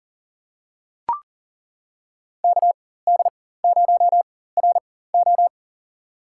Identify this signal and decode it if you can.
Morse code, heard as a keyed tone, about 40 wpm, spaced as for 19 wpm, 700 Hz. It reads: KD0RO
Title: USA Call Signs - 4x Character Spacing - 40wpm